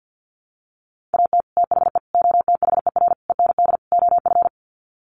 Morse code, heard as a keyed tone, about 50 wpm, 700 Hz. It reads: AT THE OTHER END OF
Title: Sets of 5 Words 50wpm